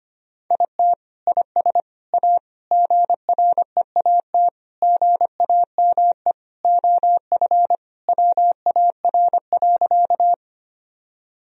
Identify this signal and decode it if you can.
Morse code, heard as a keyed tone, about 25 wpm, 700 Hz. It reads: IT IS A GREAT GAME OF WAR.
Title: Sentences from Top 1000 Words - Review Part 3 - 25wpm